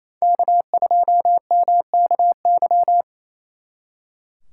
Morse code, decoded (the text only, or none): K2MKY